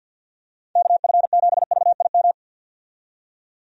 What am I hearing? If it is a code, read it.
Morse code, heard as a keyed tone, about 46 wpm, 700 Hz. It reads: KF7VIM
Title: USA Call Signs - 6 Characters - 50wpm